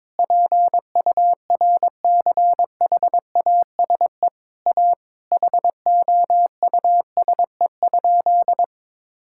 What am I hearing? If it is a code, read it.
Morse code, heard as a keyed tone, about 22 wpm, 700 Hz. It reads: PURCHASE A HOUSE?
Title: Sentences from Top 1000 Words - Review Part 5 - 22wpm